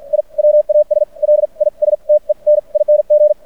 Morse code, heard as a keyed tone, about 50 wpm, spaced as for 40 wpm, 590 Hz. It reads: I WAS LISTENING
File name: archived / CRSnet